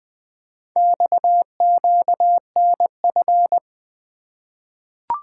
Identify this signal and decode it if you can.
Morse code, heard as a keyed tone, about 20 wpm, 700 Hz. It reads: XQNF
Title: Four Letter-Number Combinations - 20wpm